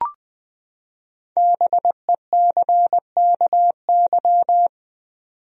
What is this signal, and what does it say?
Morse code, heard as a keyed tone, about 21 wpm, 700 Hz. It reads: BECKY
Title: USA Common Names - Max 5 Letters - 20wpm